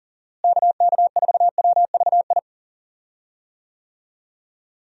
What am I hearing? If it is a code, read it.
Morse code, heard as a keyed tone, about 40 wpm, 700 Hz. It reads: KK4WVI
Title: USA Call Signs - 6 Characters - Repeated 3x - 40wpm